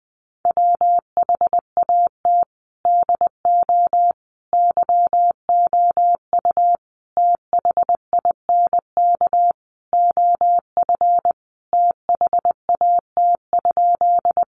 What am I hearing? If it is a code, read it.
Morse code, heard as a keyed tone, about 20 wpm, 700 Hz. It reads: WHAT DO YOU THINK OF THAT?
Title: Sentences from Top 200 Words 20wpm